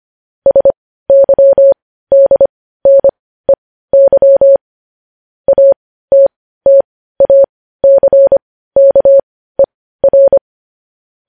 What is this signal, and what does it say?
Morse code, heard as a keyed tone, about 25 wpm, spaced as for 15 wpm, 550 Hz. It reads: SYDNEY ATTACKER
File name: podcast / News headlines-1x-25x15